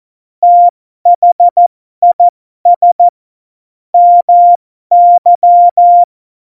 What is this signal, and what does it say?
Morse code, heard as a keyed tone, about 14 wpm, spaced as for 12 wpm, 700 Hz. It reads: THIS MY
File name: Morsekurs / Witze